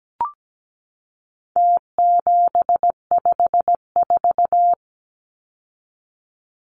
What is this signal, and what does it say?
Morse code, heard as a keyed tone, about 17 wpm, 700 Hz. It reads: T754
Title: Numbers - Contest Style - 4-Digit 17wpm